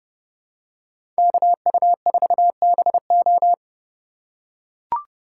Morse code, heard as a keyed tone, about 32 wpm, 700 Hz. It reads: KU4BO
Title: USA Call Signs - 5 Characters - 4x Character Spacing - 30wpm